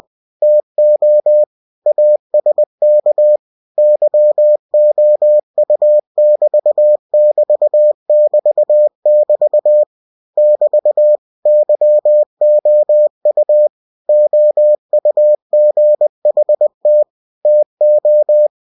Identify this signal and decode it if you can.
Morse code, heard as a keyed tone, about 20 wpm, 600 Hz. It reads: TO ASK YOU<BT><BT><BT><BT> <BT> YOU OUGHT TO
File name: cw-books / winnie-the-pooh / winnie-20wpm-page-0051